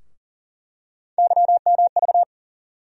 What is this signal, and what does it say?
Morse code, heard as a keyed tone, about 40 wpm, 700 Hz. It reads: YMV
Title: USA Call Sign Postfixes - 3 Letters 40wpm